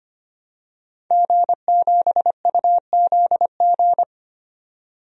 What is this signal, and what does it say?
Morse code, heard as a keyed tone, about 25 wpm, 700 Hz. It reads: G7UZG
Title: DXCC-100 Call Signs - 5 Characters - 25wpm